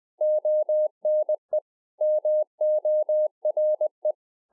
Morse code, heard as a keyed tone, about 20 wpm, 610 Hz. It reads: ONE MORE
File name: 20WPM